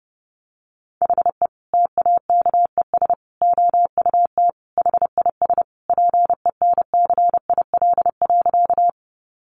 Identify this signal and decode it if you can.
Morse code, heard as a keyed tone, about 30 wpm, 700 Hz. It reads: HE TAKES OUT HIS PENCIL.
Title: Additional Easy Phrases 30wpm